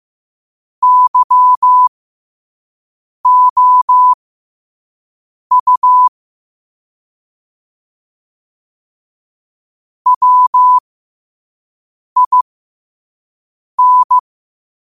Morse code, heard as a keyed tone, about 15 wpm, spaced as for 5 wpm, 1000 Hz. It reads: YOU WIN
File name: qotdfiles / 5wpm